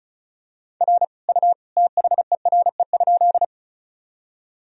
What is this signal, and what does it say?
Morse code, heard as a keyed tone, about 34 wpm, 700 Hz. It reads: R U THERE?
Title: Sentences from Top 100 Words and Abbreviations 35wpm